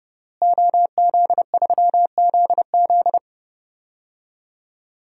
Morse code, heard as a keyed tone, about 30 wpm, 700 Hz. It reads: OZ3ZZ
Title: DXCC-100 Call Signs - 5 Characters - Repeated 3x - 30wpm